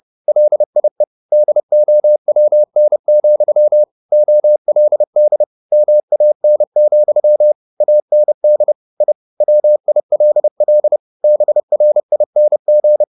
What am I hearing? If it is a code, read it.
Morse code, heard as a keyed tone, about 30 wpm, 600 Hz. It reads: LIE DOWN, OLD MAN, AND I WILL BRING